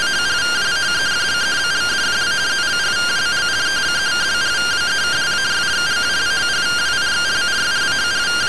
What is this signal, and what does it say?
Two-tone teleprinter signal, about 170 Hz shift.